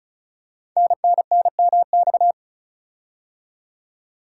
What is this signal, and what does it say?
Morse code, heard as a keyed tone, about 36 wpm, 690 Hz. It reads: NNNMX
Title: Code Groups - 5 Letters - 35wpm